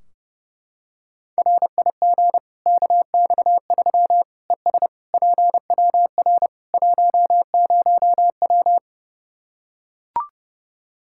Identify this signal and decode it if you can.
Morse code, heard as a keyed tone, about 32 wpm, 700 Hz. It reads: RIG KX3 ES PWR 10W
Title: Practice QSO - Exchange 2 - Info - 30wpm